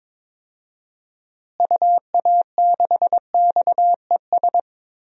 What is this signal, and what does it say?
Morse code, heard as a keyed tone, about 21 wpm, 700 Hz. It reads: UA6XES
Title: DXCC-100 Call Signs - 6 Characters - Repeated 2x - 22wpm